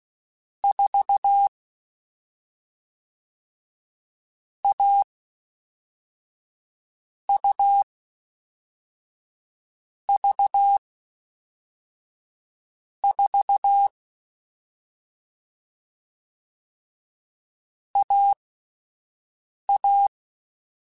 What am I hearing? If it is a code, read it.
Morse code, heard as a keyed tone, about 16 wpm, spaced as for 5 wpm, 800 Hz. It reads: 4 A U V 4 AA
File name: Difficult Combos / Difficult Combos-mp3s / AUV AUV4 @5